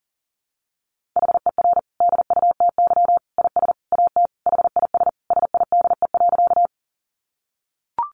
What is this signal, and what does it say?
Morse code, heard as a keyed tone, about 40 wpm, 700 Hz. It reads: HER DUTY IS AT HIS SIDE.
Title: Sentences from Top 1500 Words 40wpm